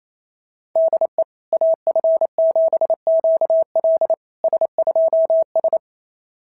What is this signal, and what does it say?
Morse code, heard as a keyed tone, about 27 wpm, 690 Hz. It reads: DE AF7QL S2S
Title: SOTA QSO Practice 28wpm